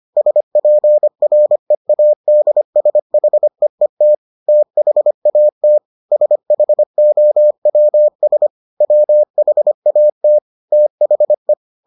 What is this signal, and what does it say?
Morse code, heard as a keyed tone, about 25 wpm, 600 Hz. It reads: SPREADSHEET THAT SHOWS WHAT THE